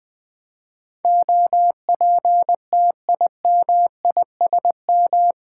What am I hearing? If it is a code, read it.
Morse code, heard as a keyed tone, about 20 wpm, 700 Hz. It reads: OPTIMISM
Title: Eight Letter Words - 20wpm